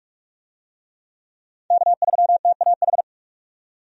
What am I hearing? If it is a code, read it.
Morse code, heard as a keyed tone, about 46 wpm, 700 Hz. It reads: K3TAH
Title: USA Call Signs - 5 Characters - 45wpm